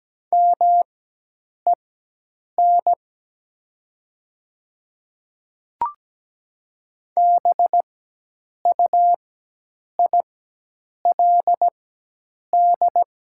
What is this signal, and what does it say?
Morse code, heard as a keyed tone, about 17 wpm, spaced as for 8 wpm, 700 Hz. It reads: MEN BUILD
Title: 4x Character Spacing - Top 200 Words - 17wpm